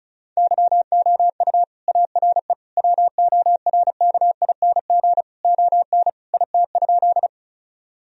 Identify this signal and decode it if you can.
Morse code, heard as a keyed tone, about 34 wpm, 700 Hz. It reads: YOU ARE WORKING ON IT?
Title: Sentences from Top 1000 Words - Review Part 4 - 35wpm